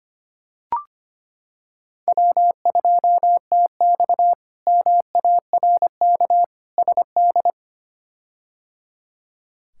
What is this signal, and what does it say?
Morse code, heard as a keyed tone, about 25 wpm, 700 Hz. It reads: W2TX MARK SD
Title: K1USN SST QSO Practice - 25wpm